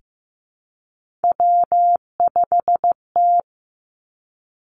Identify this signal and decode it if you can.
Morse code, heard as a keyed tone, about 15 wpm, 700 Hz. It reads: W5T
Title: Three Challenging Letter-Number Combinations 15wpm